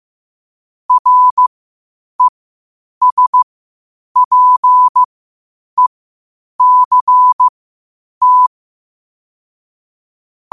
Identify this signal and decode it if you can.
Morse code, heard as a keyed tone, about 15 wpm, spaced as for 8 wpm, 1000 Hz. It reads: RESPECT
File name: qotdfiles / 8wpm